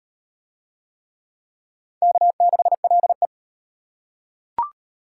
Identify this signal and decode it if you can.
Morse code, heard as a keyed tone, about 38 wpm, 700 Hz. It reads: K6LE